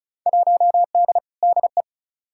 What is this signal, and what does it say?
Morse code, heard as a keyed tone, about 34 wpm, 700 Hz. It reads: 1D DE
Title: Field Day Exchanges - 35wpm